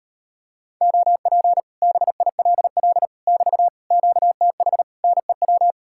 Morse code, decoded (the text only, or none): OP BILL <BT> QTH NEW